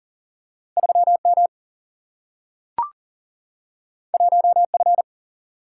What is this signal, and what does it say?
Morse code, heard as a keyed tone, about 40 wpm, 700 Hz. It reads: 3 M 1 F
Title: Single-Letter Single-Number 40wpm